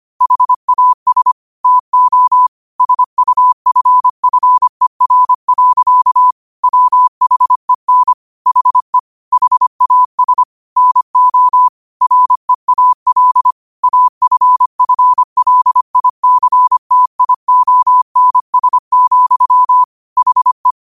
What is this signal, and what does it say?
Morse code, heard as a keyed tone, about 25 wpm, 1000 Hz. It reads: HAS TO SUFFER. WHEN HE HAS NO REAL AFFLICTIONS, HE
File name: qotdfiles / 25wpm